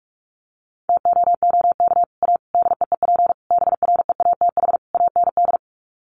Morse code, heard as a keyed tone, about 44 wpm, 700 Hz. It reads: TOOK A DEEP BREATH AND
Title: Sets of 5 Words Encore 45wpm